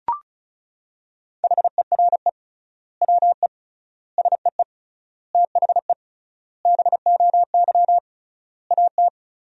Morse code, heard as a keyed tone, about 36 wpm, 700 Hz. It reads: HERE WE SEE THE BOY AT
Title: 3x Word-Spacing - Sentences from Top 300 Words - 35wpm